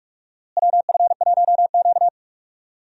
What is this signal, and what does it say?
Morse code, heard as a keyed tone, about 46 wpm, 700 Hz. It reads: WF1Q
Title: USA Call Signs - 4 Characters - Repeated 3x - 45wpm